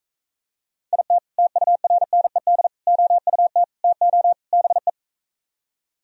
Morse code, heard as a keyed tone, about 42 wpm, 700 Hz. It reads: IT TURNED OUT TO BE